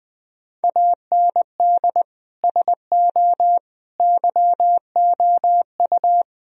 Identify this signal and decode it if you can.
Morse code, heard as a keyed tone, about 19 wpm, 700 Hz. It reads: AND SO YOU
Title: Sentences from Top 1000 Words - Review Part 1 - 20wpm